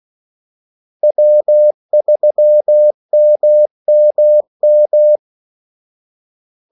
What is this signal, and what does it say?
Morse code, heard as a keyed tone, about 16 wpm, 600 Hz. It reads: W3MMM
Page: Kaldesignaler 16 wpm